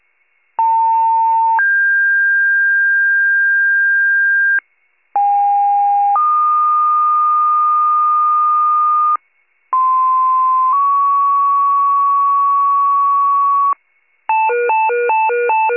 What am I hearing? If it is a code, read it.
Two-tone page, heard as a one-second tone, then a three-second tone, about 900 then 1600 Hz, about 810 then 1220 Hz, about 1040 then 1120 Hz. It is heard from 0.6 to 13.7 s.